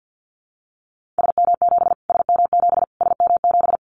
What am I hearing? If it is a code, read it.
Morse code, heard as a keyed tone, about 50 wpm, 700 Hz. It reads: SM7 SM7 SM7
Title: Callsign Prefixes for Top 25 DXCC - Repeated 3x - 50wpm